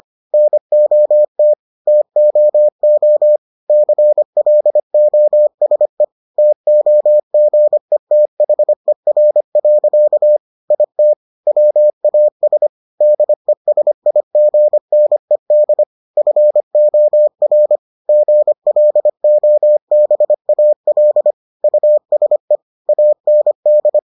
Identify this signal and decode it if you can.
Morse code, heard as a keyed tone, about 25 wpm, 600 Hz. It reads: NOT TOO CLOSE TOGETHER. IT WAS DESIGNED FOR GLOBAL USE AND